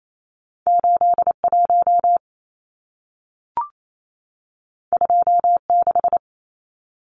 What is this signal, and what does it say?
Morse code, heard as a keyed tone, about 28 wpm, 700 Hz. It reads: 81 26
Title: Numbers - 2-Digit 28wpm